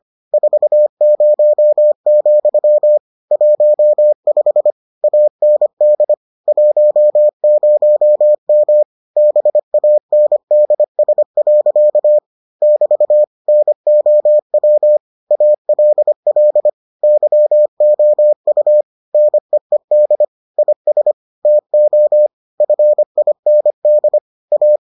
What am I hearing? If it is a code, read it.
Morse code, heard as a keyed tone, about 25 wpm, 600 Hz. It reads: 40, 15 AND 10M BANDS. <BT> NOW ALL YOU NEED IS TO FIND A